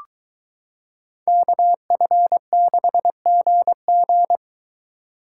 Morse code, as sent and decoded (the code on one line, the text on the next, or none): -.- ..-. -.... --. --.
KF6GG